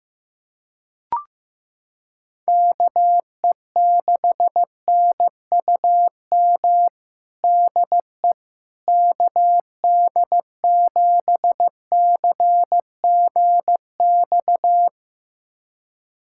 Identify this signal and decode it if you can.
Morse code, heard as a keyed tone, about 16 wpm, 700 Hz. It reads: KE6NUM DE KD7CGX
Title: USA Callsign DE Callsign - 6 Characters - 15wpm